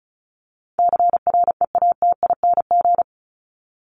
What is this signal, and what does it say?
Morse code, heard as a keyed tone, about 34 wpm, 700 Hz. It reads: CREATING
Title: ING Suffix - Top 100 Words with Base Words 35wpm